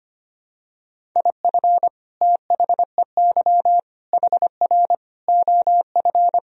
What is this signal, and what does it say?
Morse code, heard as a keyed tone, about 25 wpm, 700 Hz. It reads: IF THEY HR OF
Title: Sentences from Top 400 Words and Abbreviations 25wpm